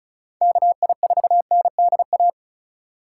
Morse code, heard as a keyed tone, about 36 wpm, 700 Hz. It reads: KI4NDA
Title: USA Call Signs - 3x Character Spacing - 35wpm